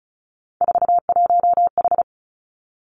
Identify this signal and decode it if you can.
Morse code, heard as a keyed tone, about 36 wpm, 700 Hz. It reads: 41H
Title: Three Challenging Letter-Number Combinations 35wpm